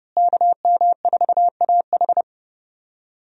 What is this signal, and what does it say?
Morse code, heard as a keyed tone, about 32 wpm, 700 Hz. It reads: KM4AH
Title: USA Call Signs - 5 Characters - Repeated 2x - 30wpm